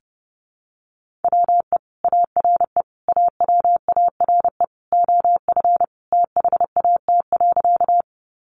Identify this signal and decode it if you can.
Morse code, heard as a keyed tone, about 30 wpm, 700 Hz. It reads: WE ARE AWARE OF THAT.
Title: Sentences from Top 1200 Words 30wpm